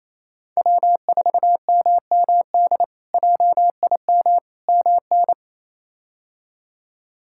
Morse code, heard as a keyed tone, about 28 wpm, 700 Hz. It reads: W4MMD JIM MN
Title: K1USN SST QSO Practice - 28wpm